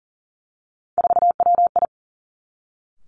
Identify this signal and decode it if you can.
Morse code, heard as a keyed tone, about 41 wpm, 700 Hz. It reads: 4WI
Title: Three Challenging Letter-Number Combinations 40wpm